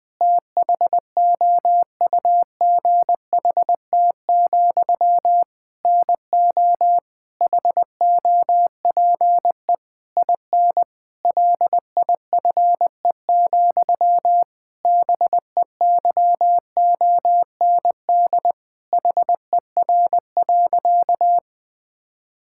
Morse code, heard as a keyed tone, about 20 wpm, 700 Hz. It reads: THOUGHT, NO HOPE IN LIFE, BEYOND HER.
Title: Sentences from Top 1000 Words - Review Part 6 - 20wpm